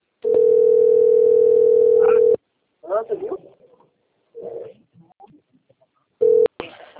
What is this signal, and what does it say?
Ringback tone (North America), 440 + 480 Hz, with speech.
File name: rg-1600-unknown- - - .52734